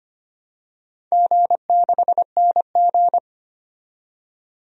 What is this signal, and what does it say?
Morse code, heard as a keyed tone, about 25 wpm, 690 Hz. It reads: G6NG
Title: DXCC-100 Call Signs - 4 Characters - 25wpm